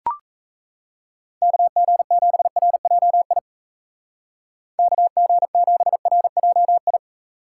Morse code, heard as a keyed tone, about 33 wpm, 700 Hz. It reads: KG7RJI KG7RJI
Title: Speed Racing - USA 2x3 Call Signs 28wpm